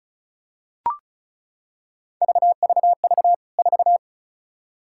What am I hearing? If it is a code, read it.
Morse code, heard as a keyed tone, about 34 wpm, 700 Hz. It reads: VVV 4
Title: K1USN SST QSO Practice 35wpm